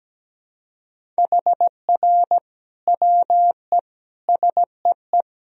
Morse code, heard as a keyed tone, about 17 wpm, 700 Hz. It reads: HR WE SEE
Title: Sentences from Top 300 Words and Abbreviations 17wpm